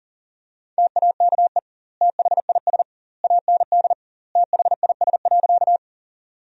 Morse code, heard as a keyed tone, about 40 wpm, 700 Hz. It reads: TAKE THIS AND THIS.
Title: 2x Word-Spacing - Sentences from Top 200 Words - 40wpm